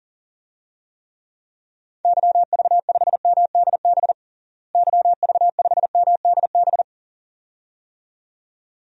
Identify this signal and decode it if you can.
Morse code, heard as a keyed tone, about 40 wpm, 700 Hz. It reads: YV5MDB YV5MDB
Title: DXCC-100 Call Signs - 6 Characters - Repeated 2x - 40wpm